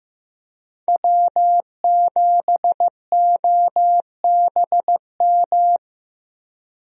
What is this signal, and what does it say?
Morse code, heard as a keyed tone, about 16 wpm, 700 Hz. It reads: W7OBM
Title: USA Call Signs - 5 Characters - Repeated 2x - 15wpm